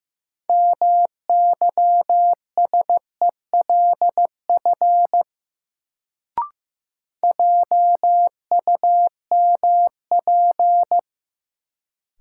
Morse code, heard as a keyed tone, about 15 wpm, 700 Hz. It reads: MYSELF JUMP
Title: Sentences from Top 1300 Words - New Word Drill Alt - 15wpm